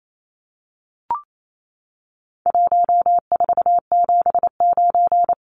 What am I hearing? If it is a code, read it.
Morse code, heard as a keyed tone, about 28 wpm, 700 Hz. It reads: 1479
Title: Numbers - Contest Style - 4-Digit 28wpm